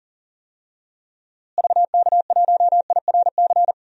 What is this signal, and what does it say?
Morse code, heard as a keyed tone, about 37 wpm, 700 Hz. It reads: VK1IRC
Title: DXCC-100 Call Signs - 6 Characters - Repeated 2x - 40wpm